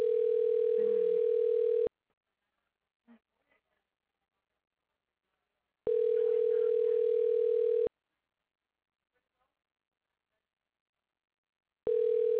Ringback tone (North America), 440 + 480 Hz.